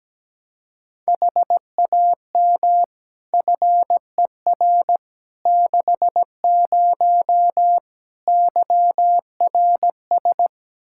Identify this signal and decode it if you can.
Morse code, heard as a keyed tone, about 17 wpm, 700 Hz. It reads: HAM FER 60 YRS
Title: Practice QSO - Exchange 3 - More Info - 17wpm